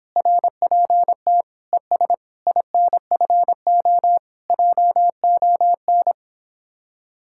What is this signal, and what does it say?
Morse code, heard as a keyed tone, about 29 wpm, 700 Hz. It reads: RPT ES INFO JON